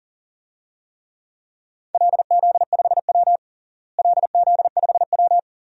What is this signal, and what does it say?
Morse code, heard as a keyed tone, about 40 wpm, 700 Hz. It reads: LZ5W LZ5W
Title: DXCC-100 Call Signs - 4 Characters - Repeated 2x - 40wpm